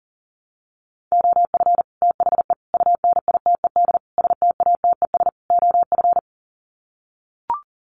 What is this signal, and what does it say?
Morse code, heard as a keyed tone, about 40 wpm, 700 Hz. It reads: OF THE UNITED STATES OF
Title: Sets of 5 Words Encore 40wpm